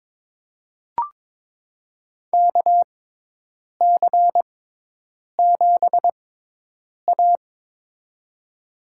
Morse code, heard as a keyed tone, about 22 wpm, spaced as for 8 wpm, 700 Hz. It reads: KC7A
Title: USA Call Signs - 4 Characters - 6x Character Spacing - 22wpm